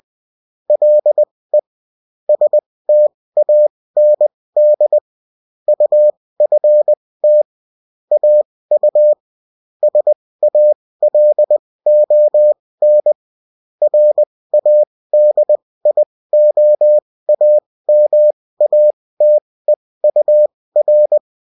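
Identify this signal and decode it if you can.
Morse code, heard as a keyed tone, about 20 wpm, spaced as for 16 wpm, 600 Hz. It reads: LE STAND UFT AU SALON RADIOAMATEUR
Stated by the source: Morse code bulletin of a French amateur CW union, sent as text and in CW.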